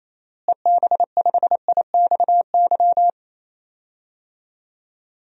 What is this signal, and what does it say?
Morse code, heard as a keyed tone, about 28 wpm, 700 Hz. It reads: EB5IXY
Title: DXCC-100 Call Signs - 6 Characters - Repeated 3x - 28wpm